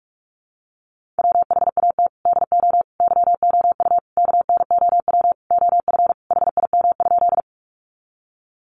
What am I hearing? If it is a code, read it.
Morse code, heard as a keyed tone, about 44 wpm, 700 Hz. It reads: WHAT DO YOU KNOW OF HIM?
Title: Sentences from Top 100 Words 45wpm